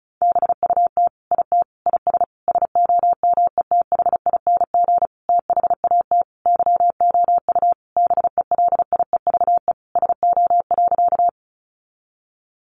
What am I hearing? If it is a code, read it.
Morse code, heard as a keyed tone, about 34 wpm, 700 Hz. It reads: BUT IT IS SOMETHING THAT YOU BELIEVE SO.
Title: Sentences from Top 1000 Words 35wpm